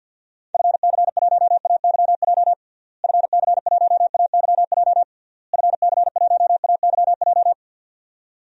Morse code, heard as a keyed tone, about 51 wpm, 700 Hz. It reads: FC1AYJ FC1AYJ FC1AYJ
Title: DXCC-100 Call Signs - 6 Characters - Repeated 3x - 50wpm